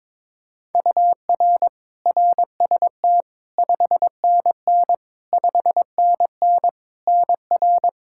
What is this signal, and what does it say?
Morse code, heard as a keyed tone, about 22 wpm, 700 Hz. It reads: UR RST 5NN 5NN NR
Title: Practice QSO - Exchange 1 - Beginning - 22wpm